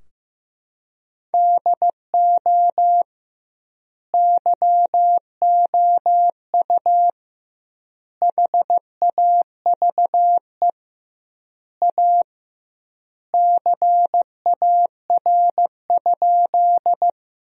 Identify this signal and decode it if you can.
Morse code, heard as a keyed tone, about 15 wpm, 700 Hz. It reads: DO YOU HAVE A CAR?
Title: 2x Word-Spacing - Sentences from Top 300 Words - 15wpm